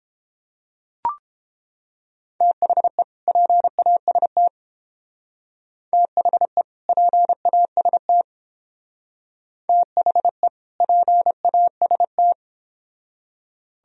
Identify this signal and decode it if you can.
Morse code, heard as a keyed tone, about 31 wpm, 700 Hz. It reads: THE PAST THE PAST THE PAST